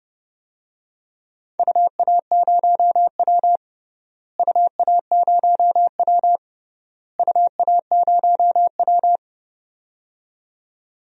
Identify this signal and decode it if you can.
Morse code, heard as a keyed tone, about 30 wpm, 700 Hz. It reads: UA0W UA0W UA0W
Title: DXCC-100 Call Signs - 4 Characters - Repeated 3x - 30wpm